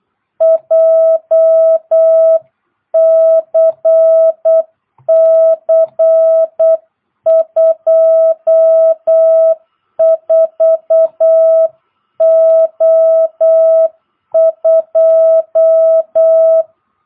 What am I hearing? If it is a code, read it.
Morse code, heard as a keyed tone, about 8 wpm, 660 Hz. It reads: JCC 24O2
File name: jh8cbh / 2022cw